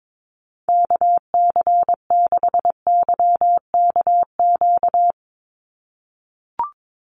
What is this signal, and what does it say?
Morse code, heard as a keyed tone, about 22 wpm, 700 Hz. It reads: KC6YKQ
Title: USA Call Signs 22wpm